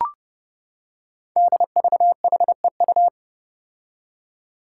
Morse code, heard as a keyed tone, about 29 wpm, 700 Hz. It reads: DVHEU
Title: Five Challenging Letter-Number Combinations 30wpm